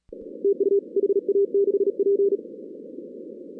Morse code, heard as a keyed tone, about 39 wpm, 410 Hz. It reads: TU HA6P
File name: 2016 arrl dx cw / 2016-ARRL-DX-CW-K5TR- - 0-46